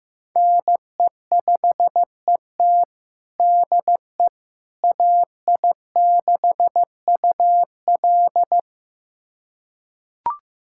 Morse code, heard as a keyed tone, about 15 wpm, 700 Hz. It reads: NE5ET DE AI6UL
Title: USA Callsign DE Callsign - 5 Characters - 15wpm